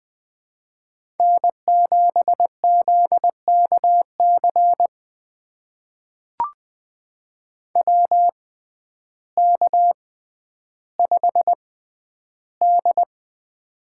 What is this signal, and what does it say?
Morse code, heard as a keyed tone, about 20 wpm, spaced as for 11 wpm, 700 Hz. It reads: N7ZKC WK5D
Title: USA Call Signs - 6x Character Spacing - 20wpm